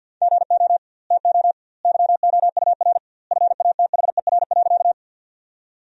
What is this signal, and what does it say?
Morse code, heard as a keyed tone, about 51 wpm, 700 Hz. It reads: GO TO YOUR FATHER.
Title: 2x Word-Spacing - Sentences from Top 300 Words - 50wpm